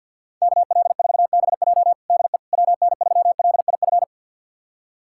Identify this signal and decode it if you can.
Morse code, heard as a keyed tone, about 52 wpm, 700 Hz. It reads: KL4BJ DE WN3LIF
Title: USA Callsign DE Callsign 50wpm